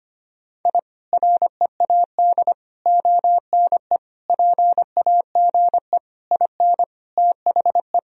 Morse code, heard as a keyed tone, about 25 wpm, 700 Hz. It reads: I READ ONE PAGE IN THE
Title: Sentences from Top 1000 Words - Review Part 1 - 25wpm